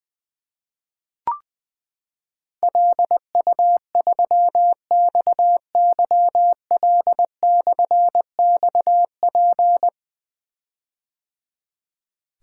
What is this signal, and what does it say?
Morse code, heard as a keyed tone, about 20 wpm, 700 Hz. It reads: LU3XYL/XP
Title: Difficult Call Signs 20wpm